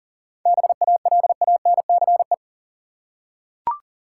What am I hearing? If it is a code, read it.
Morse code, heard as a keyed tone, about 40 wpm, 700 Hz. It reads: BALANCE
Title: Seven Letter Words - 40wpm